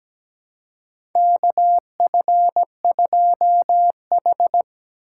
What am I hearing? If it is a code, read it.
Morse code, heard as a keyed tone, about 17 wpm, 700 Hz. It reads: KF2H
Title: USA Call Signs - 4 Characters - 8x Character Spacing - 17wpm